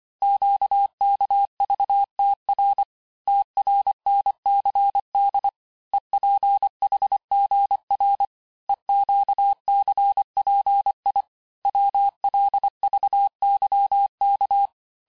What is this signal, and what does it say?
Morse code, heard as a keyed tone, about 25 wpm, 800 Hz. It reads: QKVTR TRNCD EPHGR EQCPI WLVYK
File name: texte-chiffre-vitesse-23mpm